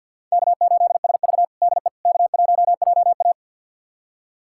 Morse code, heard as a keyed tone, about 49 wpm, 700 Hz. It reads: K8SV DE K1JA
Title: USA Callsign DE Callsign - 4 Characters - 50wpm